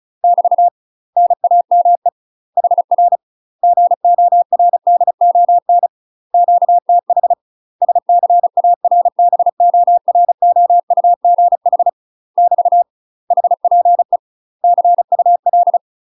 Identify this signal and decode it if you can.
Morse code, heard as a keyed tone, about 36 wpm, 700 Hz. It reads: <BT> NAME HR GORDON QTH SCARBOROUGH <BT> HPE CUL
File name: UK MASTER QSOs TRAINING / 35WPM / UK MASTER QSOs 0041